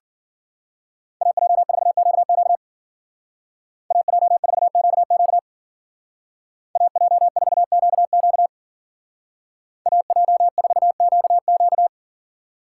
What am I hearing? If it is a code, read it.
Morse code, heard as a keyed tone, about 48 wpm, 700 Hz. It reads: AJ4QQ AJ4QQ AJ4QQ AJ4QQ
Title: Speed Racing - USA 2x2 Call Signs 40wpm